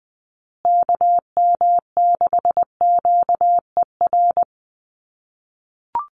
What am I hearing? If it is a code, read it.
Morse code, heard as a keyed tone, about 20 wpm, 700 Hz. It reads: KM6QER
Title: USA Call Signs Repeated 3x 20wpm